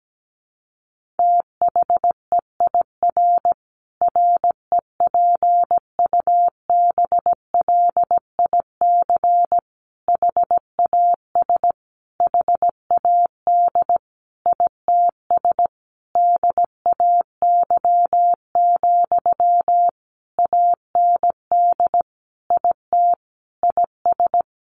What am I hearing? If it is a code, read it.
Morse code, heard as a keyed tone, about 17 wpm, 700 Hz. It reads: THEIR REPUBLIC HAS HAD ITS DAY, AND IT IS
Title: Sentences from Top 1500 Words 17wpm